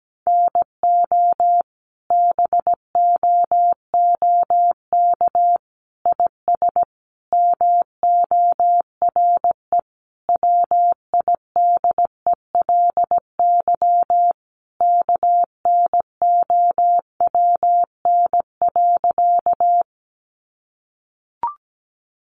Morse code, heard as a keyed tone, about 17 wpm, 700 Hz. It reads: NO BOOK IS MORE WIDELY KNOWN.
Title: Sentences from Top 1200 Words 17wpm